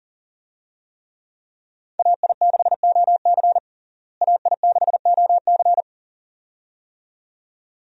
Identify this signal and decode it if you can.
Morse code, heard as a keyed tone, about 40 wpm, 700 Hz. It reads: AI6OC AI6OC
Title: USA Call Signs - 5 Characters - Repeated 2x - 40wpm